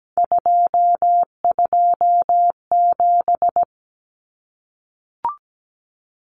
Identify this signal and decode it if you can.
Morse code, heard as a keyed tone, about 17 wpm, 700 Hz. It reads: 227
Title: Common Cut Numbers - 3-Digit 17wpm